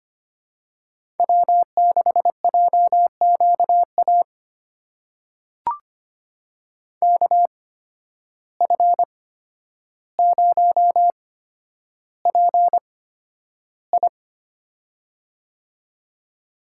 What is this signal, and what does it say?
Morse code, heard as a keyed tone, about 25 wpm, spaced as for 11 wpm, 700 Hz. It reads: W6JQA KF0PI
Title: USA Call Signs - 8x Character Spacing - 25wpm